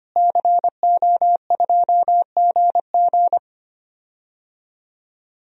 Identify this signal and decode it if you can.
Morse code, heard as a keyed tone, about 25 wpm, 700 Hz. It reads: CO2GG
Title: DXCC-100 Call Signs - 5 Characters - Repeated 3x - 25wpm